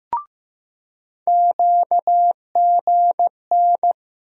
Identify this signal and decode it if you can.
Morse code, heard as a keyed tone, about 15 wpm, 700 Hz. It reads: QGN
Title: USA Call Sign Postfixes - 3 Letters 15wpm